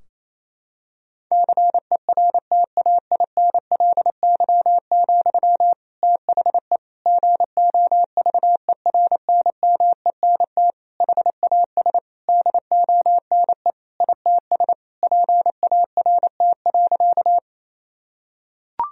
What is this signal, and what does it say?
Morse code, heard as a keyed tone, about 28 wpm, 700 Hz. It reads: CERTAINLY, THE GOVERNMENT HAS DONE ITS PART.
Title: Sentences from Top 1000 Words - Review Part 3 - 28wpm